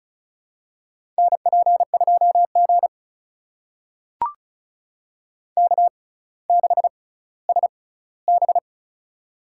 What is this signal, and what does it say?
Morse code, heard as a keyed tone, about 36 wpm, spaced as for 18 wpm, 700 Hz. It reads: NP2G K6SB
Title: USA Call Signs - 4 Characters - 6x Character Spacing - 35wpm